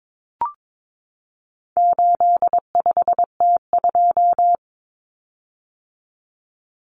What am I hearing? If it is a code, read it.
Morse code, heard as a keyed tone, about 22 wpm, 700 Hz. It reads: 85T2
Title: Common Cut Numbers - 4-Digit 22wpm